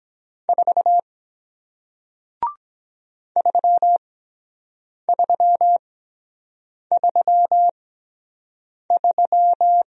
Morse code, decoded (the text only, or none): 4 3333